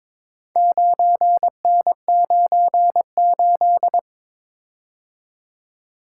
Morse code, heard as a keyed tone, about 22 wpm, 700 Hz. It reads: 9N98
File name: practice / 22wpm / Four Letter-Number Combinations - 22wpm